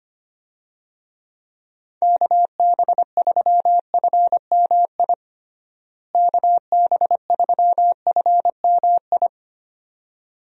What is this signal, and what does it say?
Morse code, heard as a keyed tone, about 25 wpm, 700 Hz. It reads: KB3FMI KB3FMI
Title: USA Call Signs - 6 Characters - Repeated 2x - 25wpm